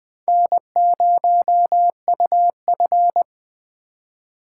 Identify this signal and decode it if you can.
Morse code, heard as a keyed tone, about 24 wpm, 700 Hz. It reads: N0UF
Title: Speed Racing - USA 1x2 Call Signs 20wpm